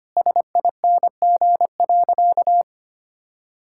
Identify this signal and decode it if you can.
Morse code, heard as a keyed tone, about 25 wpm, 700 Hz. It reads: SING.
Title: Sentences from Top 1600 Words 25wpm